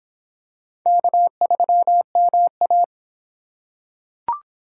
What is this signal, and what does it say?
Morse code, heard as a keyed tone, about 25 wpm, 700 Hz. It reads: K3MA